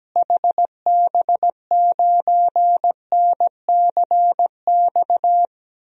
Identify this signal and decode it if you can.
Morse code, heard as a keyed tone, about 17 wpm, 700 Hz. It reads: HB9NCX
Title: DXCC-100 Call Signs - 6 Characters - Repeated 2x - 17wpm